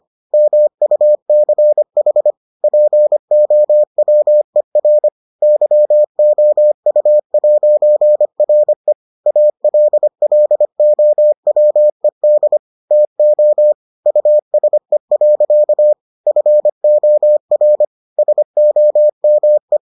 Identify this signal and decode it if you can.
Morse code, heard as a keyed tone, about 25 wpm, 600 Hz. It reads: MUCH POWER YOU'RE ALLOWED TO USE. FOR SOME